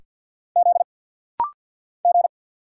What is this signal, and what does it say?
Morse code, heard as a keyed tone, about 50 wpm, 690 Hz. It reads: Z G
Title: Single Letter-Number Alt - Rapid-Fire 50wpm